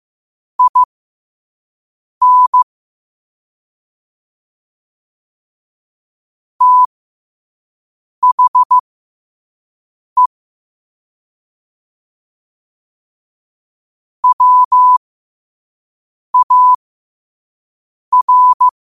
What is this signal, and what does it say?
Morse code, heard as a keyed tone, about 15 wpm, spaced as for 5 wpm, 1000 Hz. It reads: IN THE WAR
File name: qotdfiles / 5wpm